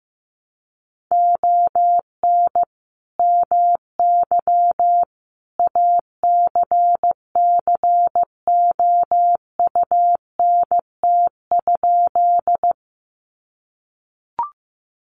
Morse code, heard as a keyed tone, about 15 wpm, 700 Hz. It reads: ON MY ACCOUNT?
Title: Sentences from Top 900 Words 15wpm